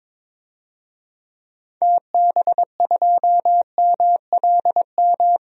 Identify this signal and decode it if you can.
Morse code, heard as a keyed tone, about 22 wpm, 700 Hz. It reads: TB2MLM
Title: DXCC-100 Call Signs - 6 Characters - 22wpm